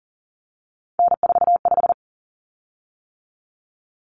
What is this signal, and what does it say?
Morse code, heard as a keyed tone, about 40 wpm, 700 Hz. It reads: N45
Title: Common Cut Numbers - 3-Digit 40wpm